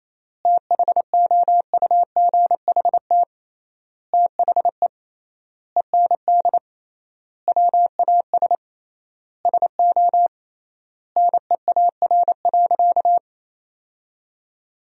Morse code, heard as a keyed tone, about 28 wpm, 700 Hz. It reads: THOUGHT THE END WAS SO NEAR.
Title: 3x Word-Spacing - Sentences from Top 300 Words - 28wpm